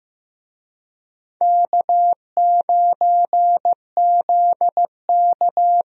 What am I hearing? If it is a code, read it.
Morse code, heard as a keyed tone, about 16 wpm, 700 Hz. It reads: K9ZK
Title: USA Call Signs - 4 Characters - Repeated 2x - 15wpm